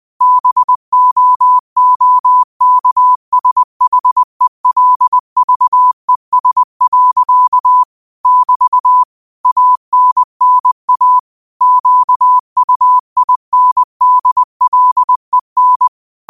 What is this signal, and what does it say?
Morse code, heard as a keyed tone, about 20 wpm, 1000 Hz. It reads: BOOKSHELVES. <BT> ANNA QUINDLEN